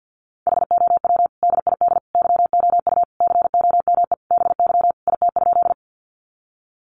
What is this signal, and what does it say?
Morse code, heard as a keyed tone, about 50 wpm, 700 Hz. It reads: HOW DID YOU COME BY IT?
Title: Sentences from Top 100 Words 50wpm